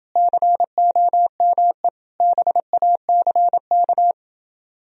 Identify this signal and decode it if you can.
Morse code, heard as a keyed tone, about 28 wpm, 700 Hz. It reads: COME BACK